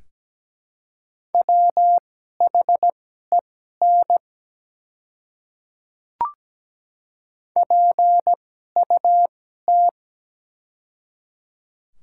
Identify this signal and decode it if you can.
Morse code, heard as a keyed tone, about 17 wpm, spaced as for 12 wpm, 700 Hz. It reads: WHEN PUT
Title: 2x Character Spacing - Top 200 Words - 17wpm